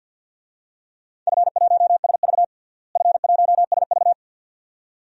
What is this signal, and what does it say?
Morse code, heard as a keyed tone, about 50 wpm, 700 Hz. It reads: F1SV F1SV
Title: DXCC-100 Call Signs - 4 Characters - Repeated 2x - 50wpm